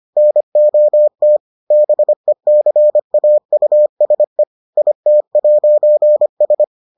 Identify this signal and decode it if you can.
Morse code, heard as a keyed tone, about 25 wpm, 600 Hz. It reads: NOT BECAUSE IT'S